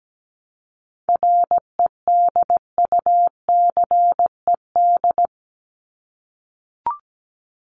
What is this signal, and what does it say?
Morse code, heard as a keyed tone, about 17 wpm, 700 Hz. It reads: REDUCED
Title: ED Suffix - Top 100 Words with Base Words 17wpm